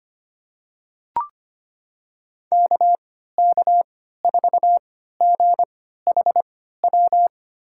Morse code, heard as a keyed tone, about 25 wpm, spaced as for 14 wpm, 700 Hz. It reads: KK4GHW
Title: USA Call Signs - 6 Characters - 3x Character Spacing - 25wpm